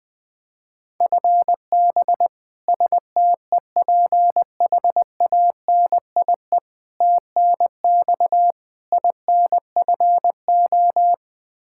Morse code, heard as a keyed tone, about 20 wpm, 700 Hz. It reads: FB STEPHANIE TNX INFO
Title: Practice QSO - Exchange 3 - More Info - 20wpm